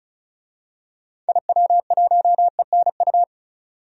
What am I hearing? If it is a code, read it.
Morse code, heard as a keyed tone, about 36 wpm, 700 Hz. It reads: IW1ENU
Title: DXCC-100 Call Signs - 6 Characters - 35wpm